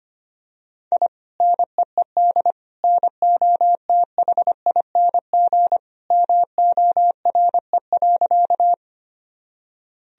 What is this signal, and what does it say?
Morse code, heard as a keyed tone, about 25 wpm, 700 Hz. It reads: I NEED NOTHING MORE.
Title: Sentences from Top 1000 Words - Review Part 1 - 25wpm